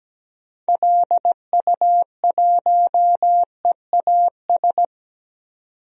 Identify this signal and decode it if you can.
Morse code, heard as a keyed tone, about 17 wpm, 700 Hz. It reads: LU1EAS
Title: DXCC-100 Call Signs - 6 Characters - 17wpm